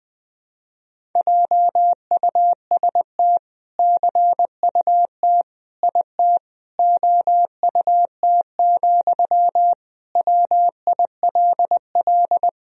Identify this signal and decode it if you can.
Morse code, heard as a keyed tone, about 20 wpm, 700 Hz. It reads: JUST CUT IT OUT, WILL
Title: Sentences from Top 1000 Words - Review Part 1 - 20wpm